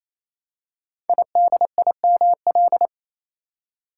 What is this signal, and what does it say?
Morse code, heard as a keyed tone, about 29 wpm, 700 Hz. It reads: IDIML
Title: Code Groups - 5 Letter:Numbers - 28wpm